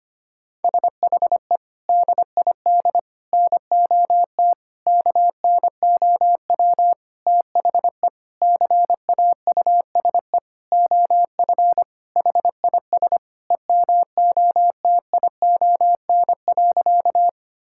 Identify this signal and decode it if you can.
Morse code, heard as a keyed tone, about 25 wpm, 700 Hz. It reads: SHE DID NOT KNOW THE CAUSE OF HIS EMOTION.
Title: Sentences from Top 1600 Words 25wpm